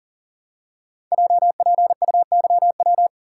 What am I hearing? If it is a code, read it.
Morse code, heard as a keyed tone, about 38 wpm, 700 Hz. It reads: JPUYW
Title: Code Groups - 5 Letters - 40wpm